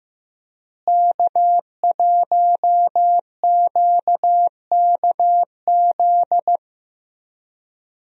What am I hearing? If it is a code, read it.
Morse code, heard as a keyed tone, about 15 wpm, 700 Hz. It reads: K1QKZ
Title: USA Call Signs - 5 Characters - Repeated 2x - 15wpm